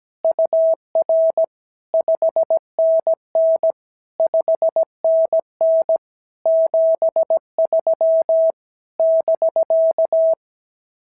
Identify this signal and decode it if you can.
Morse code, heard as a keyed tone, about 17 wpm, 650 Hz. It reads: UR 5NN 5NN 73 <BK>
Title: SOTA QSO Practice 17wpm